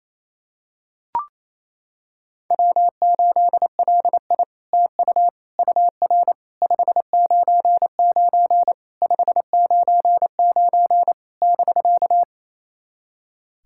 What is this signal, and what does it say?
Morse code, heard as a keyed tone, about 28 wpm, 700 Hz. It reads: W8LI TU UR 599 599 <BK>
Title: SOTA QSO Practice 28wpm